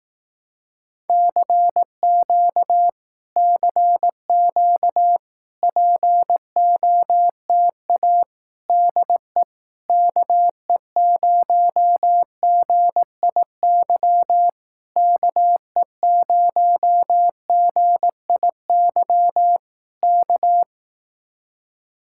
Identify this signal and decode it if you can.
Morse code, heard as a keyed tone, about 18 wpm, 700 Hz. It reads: CQ CQ POTA DE KE0GIY KE0GIY K
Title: Speed-Racing Common QSO Phrases Encore - 15wpm